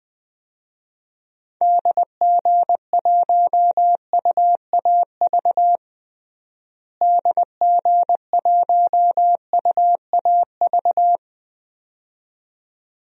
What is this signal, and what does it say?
Morse code, heard as a keyed tone, about 20 wpm, 700 Hz. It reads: DG1UAV DG1UAV
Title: DXCC-100 Call Signs - 6 Characters - Repeated 2x - 20wpm